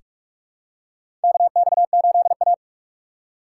Morse code, heard as a keyed tone, about 45 wpm, 700 Hz. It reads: KX8A
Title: USA Call Signs - 6x Character Spacing - 45wpm